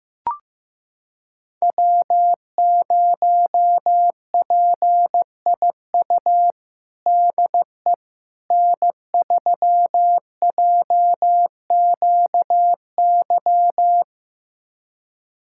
Morse code, heard as a keyed tone, about 15 wpm, 700 Hz. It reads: W0PIU DE N3JQY
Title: USA Callsign DE Callsign - 5 Characters - 15wpm